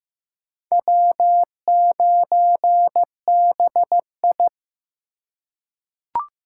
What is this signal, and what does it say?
Morse code, heard as a keyed tone, about 15 wpm, 700 Hz. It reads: W9BI
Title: USA Call Signs - 4 Characters - 4x Character Spacing - 15wpm